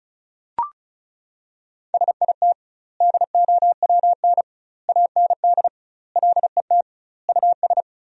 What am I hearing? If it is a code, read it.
Morse code, heard as a keyed tone, about 34 wpm, 700 Hz. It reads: SIT DOWN AND LET US
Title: 2x Word-Spacing - Sentences from Top 500 Words - 35wpm